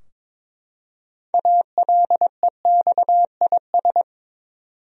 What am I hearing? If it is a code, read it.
Morse code, heard as a keyed tone, about 22 wpm, 700 Hz. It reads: ALEXIS
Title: USA Common Names - 22wpm